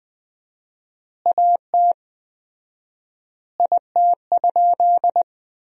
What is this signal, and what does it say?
Morse code, heard as a keyed tone, about 21 wpm, 700 Hz. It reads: AT IT?
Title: 4x Word-Spacing - Sentences from Top 100 Words - 20wpm